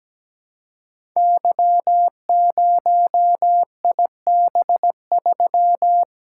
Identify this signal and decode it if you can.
Morse code, heard as a keyed tone, about 17 wpm, 700 Hz. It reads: Y0IB3
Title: Code Groups - 5 Letter:Numbers - 17wpm